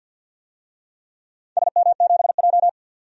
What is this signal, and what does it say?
Morse code, heard as a keyed tone, about 51 wpm, 700 Hz. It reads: SM7J
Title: DXCC-100 Call Signs - 4 Characters - 50wpm